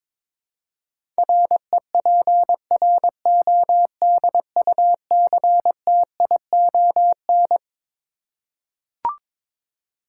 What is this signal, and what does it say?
Morse code, heard as a keyed tone, about 22 wpm, 700 Hz. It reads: REPRODUCTION
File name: Twelve Letter Words - 22wpm